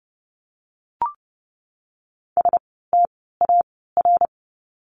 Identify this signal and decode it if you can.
Morse code, heard as a keyed tone, about 30 wpm, 700 Hz. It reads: S T A R
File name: Lesson 16f - Letter c - Words 30wpm